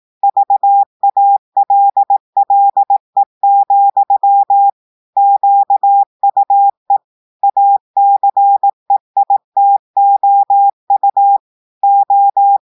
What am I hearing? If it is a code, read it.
Morse code, heard as a keyed tone, about 18 wpm, 800 Hz. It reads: VALLE, QUE ACEITOU O
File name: audios / portugais / portugais texte en clair 1 vitesse 18 mots minutes